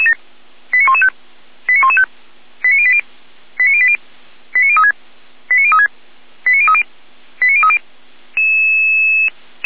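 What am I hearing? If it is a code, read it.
Five-tone selective call (ZVEI), heard as quick runs of five tones, tones of about 70 ms. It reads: 89115 89115 89080 89080 89036 89036 89030 89030
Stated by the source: ZVEI five-tone selective call received by SDR, fire brigade (Feuerwehr) Orlamünde.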